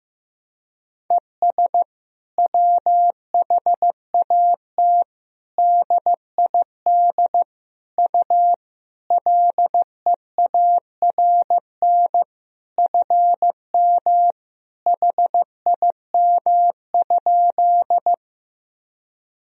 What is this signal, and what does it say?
Morse code, heard as a keyed tone, about 15 wpm, 700 Hz. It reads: ES WHAT DID U LEARN FM HIM?
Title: Sentences from Top 300 Words and Abbreviations 15wpm